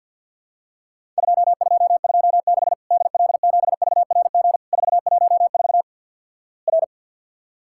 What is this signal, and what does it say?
Morse code, heard as a keyed tone, about 50 wpm, 700 Hz. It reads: 2226 DL7VRG 414 R
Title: Worked All Europe DX Contest - QTC Exchange - 50wpm